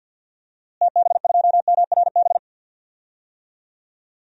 Morse code, heard as a keyed tone, about 53 wpm, 700 Hz. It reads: TB2MRB
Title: DXCC-100 Call Signs - 6 Characters - Repeated 2x - 50wpm